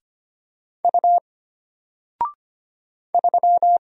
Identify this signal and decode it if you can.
Morse code, heard as a keyed tone, about 25 wpm, 700 Hz. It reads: U 3
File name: Single-Letter-Number-25wpm